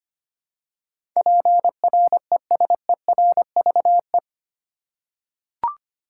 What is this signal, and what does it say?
Morse code, heard as a keyed tone, about 25 wpm, 700 Hz. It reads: PRESERVE
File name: Eight Letter Words - 25wpm